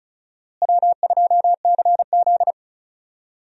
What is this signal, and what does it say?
Morse code, heard as a keyed tone, about 34 wpm, 700 Hz. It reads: W2CZ
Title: USA Call Signs - 4 Characters - Repeated 2x - 35wpm